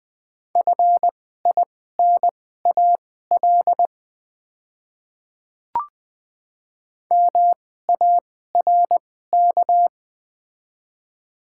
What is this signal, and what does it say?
Morse code, heard as a keyed tone, about 20 wpm, spaced as for 15 wpm, 690 Hz. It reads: FINAL MARK
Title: 2x Character Spacing - Top 500 Words - 20wpm